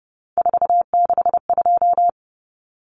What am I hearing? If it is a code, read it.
Morse code, heard as a keyed tone, about 30 wpm, 700 Hz. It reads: 462
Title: Common Cut Numbers - 3-Digit 30wpm